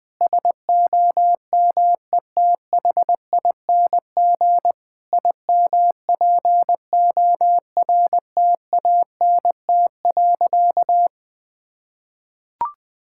Morse code, heard as a keyed tone, about 20 wpm, 700 Hz. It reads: SOMETHING IMPORTANT.
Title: Sentences from Top 1000 Words - Review Part 6 - 20wpm